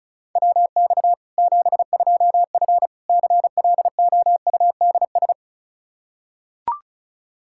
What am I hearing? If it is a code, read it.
Morse code, heard as a keyed tone, about 36 wpm, 700 Hz. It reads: WX 72F CLOUDS
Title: K1USN SST QSO Practice 35wpm